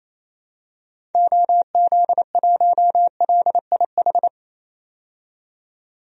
Morse code, heard as a keyed tone, about 28 wpm, 700 Hz. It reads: OZ1LIH
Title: DXCC-100 Call Signs - 6 Characters - 28wpm